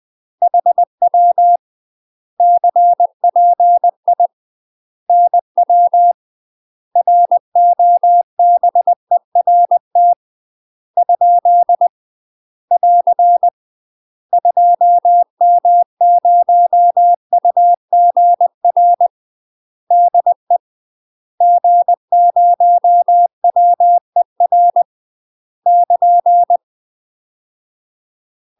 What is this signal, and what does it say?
Morse code, heard as a keyed tone, about 20 wpm, 700 Hz. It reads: HW CPI NW ROBERT ? <AR> 2M0UGR DE G0WER <KN>
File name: UK MASTER QSOs TRAINING / 20WPM / UK MASTER QSOs 0070